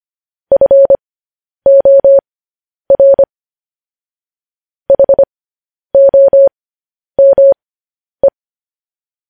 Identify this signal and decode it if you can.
Morse code, heard as a keyed tone, about 25 wpm, spaced as for 10 wpm, 550 Hz. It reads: FOR HOME
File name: podcast / News headlines-1x-25x10